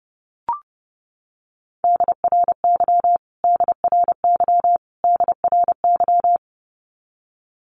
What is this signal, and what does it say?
Morse code, heard as a keyed tone, about 30 wpm, 700 Hz. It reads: DRY DRY DRY
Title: Top 500 Words Repeated 30wpm